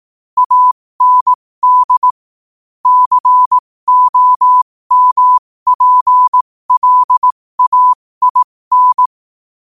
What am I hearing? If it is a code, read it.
Morse code, heard as a keyed tone, about 18 wpm, spaced as for 15 wpm, 1000 Hz. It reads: AND COMPLAIN
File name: qotdfiles / 15wpm